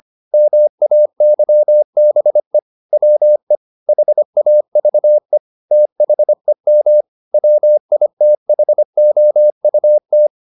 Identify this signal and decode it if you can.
Morse code, heard as a keyed tone, about 25 wpm, 600 Hz. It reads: MAYBE WE HAVE THEM WITHOUT